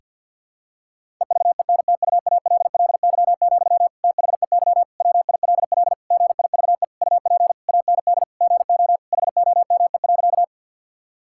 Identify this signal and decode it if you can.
Morse code, heard as a keyed tone, about 50 wpm, 700 Hz. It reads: EVENTUALLY, THEY WILL GIVE UP AND GO HOME.
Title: Sentences from Top 1000 Words - Review Part 6 - 50wpm